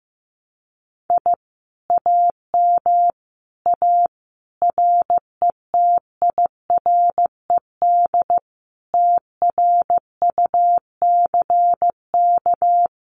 Morse code, heard as a keyed tone, about 15 wpm, 700 Hz. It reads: I AM A RETIRED TRUCK
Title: Common QSO Phrases Encore 15wpm